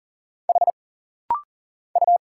Morse code, heard as a keyed tone, about 40 wpm, 700 Hz. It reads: H U
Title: Single Letters - Rapid-Fire 40wpm